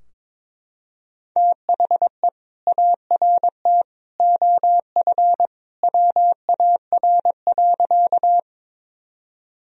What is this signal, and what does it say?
Morse code, heard as a keyed tone, about 22 wpm, 700 Hz. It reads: THE ART OF WAR.
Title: Sentences from Top 1000 Words - Review Part 3 - 22wpm